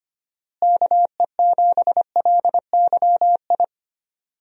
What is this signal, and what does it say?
Morse code, heard as a keyed tone, about 25 wpm, 700 Hz. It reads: KE7LYI
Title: USA Call Signs - 6 Characters - Repeated 2x - 25wpm